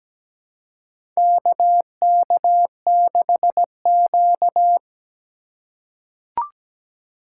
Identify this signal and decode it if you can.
Morse code, heard as a keyed tone, about 17 wpm, 700 Hz. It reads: KK6Q
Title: USA Call Signs - 4 Characters - 4x Character Spacing - 17wpm